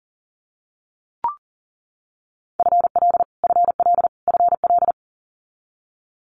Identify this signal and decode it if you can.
Morse code, heard as a keyed tone, about 37 wpm, 700 Hz. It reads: FL FL FL
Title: Multi-letter Phonemes Repeated 3x 40wpm